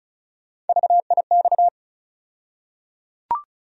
Morse code, heard as a keyed tone, about 36 wpm, 700 Hz. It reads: VIX
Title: USA Call Sign Postfixes - 3 Letters 35wpm